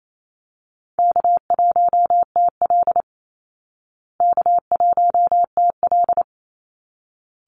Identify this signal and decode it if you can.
Morse code, heard as a keyed tone, about 28 wpm, 700 Hz. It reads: K1TL K1TL
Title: Easy Call Signs Repeated 2x - 28wpm